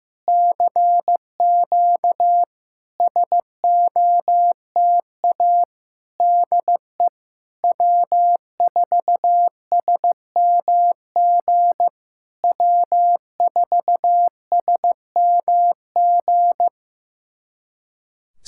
Morse code, heard as a keyed tone, about 15 wpm, 700 Hz. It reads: CQ SOTA DE W4SMG W4SMG
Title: SOTA QSO Practice 15wpm